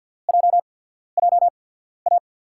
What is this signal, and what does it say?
Morse code, heard as a keyed tone, about 50 wpm, spaced as for 14 wpm, 700 Hz. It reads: JJA
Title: USA Call Signs - 6 Characters - 8x Character Spacing - 50wpm